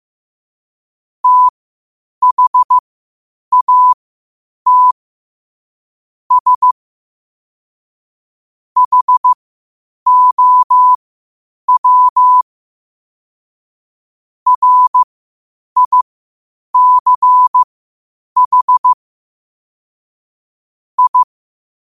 Morse code, heard as a keyed tone, about 15 wpm, spaced as for 8 wpm, 1000 Hz. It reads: THAT S HOW RICH I
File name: qotdfiles / 8wpm